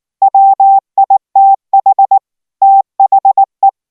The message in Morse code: WITH THE

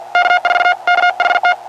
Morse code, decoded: K4KHT

